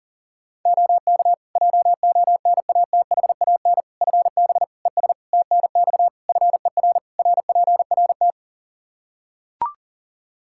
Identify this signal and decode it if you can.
Morse code, heard as a keyed tone, about 40 wpm, 700 Hz. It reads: OK JONATHAN FB ES TNX FER RPRT
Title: Practice QSO - Exchange 2 - Info - 40wpm